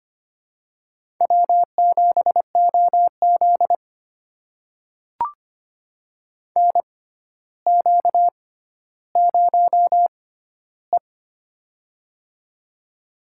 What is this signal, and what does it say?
Morse code, heard as a keyed tone, about 25 wpm, spaced as for 13 wpm, 700 Hz. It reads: W7OZ NQ0E
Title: USA Call Signs - 4 Characters - 6x Character Spacing - 25wpm